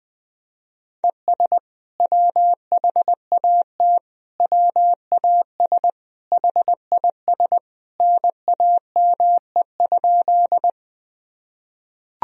Morse code, heard as a keyed tone, about 20 wpm, 700 Hz. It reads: ES WHAT WAS HIS NAME?
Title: Sentences from Top 200 Words and Abbreviations 20wpm